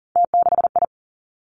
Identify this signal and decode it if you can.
Morse code, heard as a keyed tone, about 38 wpm, 690 Hz. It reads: T6I
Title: Three Challenging Letter-Number Combinations 40wpm